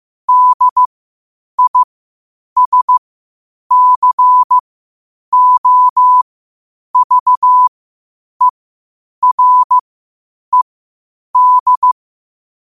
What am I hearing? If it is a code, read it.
Morse code, heard as a keyed tone, about 15 wpm, spaced as for 8 wpm, 1000 Hz. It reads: DISCOVERED
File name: qotdfiles / 8wpm